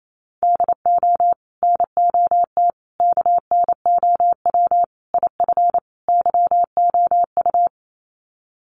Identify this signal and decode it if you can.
Morse code, heard as a keyed tone, about 28 wpm, 700 Hz. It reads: DO NOT KNOW IF YOU
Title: Sets of 5 Words 28wpm